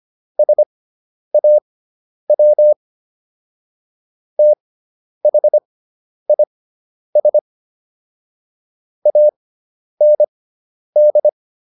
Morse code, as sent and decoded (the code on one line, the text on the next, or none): ... .- .-- / - .... .. ... / .- -. -..
SAW THIS AND